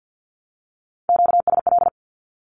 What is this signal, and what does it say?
Morse code, heard as a keyed tone, about 48 wpm, 700 Hz. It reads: QSL
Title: Q-Codes Repeated 3x 50wpm